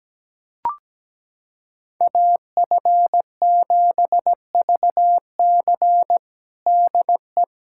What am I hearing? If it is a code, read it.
Morse code, heard as a keyed tone, about 17 wpm, 700 Hz. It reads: AF7VC DE
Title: USA Callsign DE Callsign - 5 Characters - 17wpm